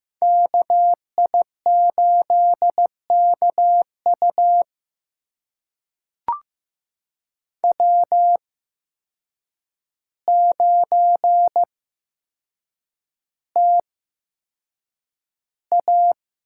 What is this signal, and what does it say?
Morse code, heard as a keyed tone, about 15 wpm, spaced as for 7 wpm, 700 Hz. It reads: KI8KU W9TA
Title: USA Call Signs - 5 Characters - 8x Character Spacing - 15wpm